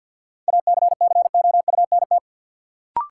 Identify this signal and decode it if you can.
Morse code, heard as a keyed tone, about 48 wpm, 700 Hz. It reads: ACCOUNT
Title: Sentences from Top 900 Words - New Word Drill - 50wpm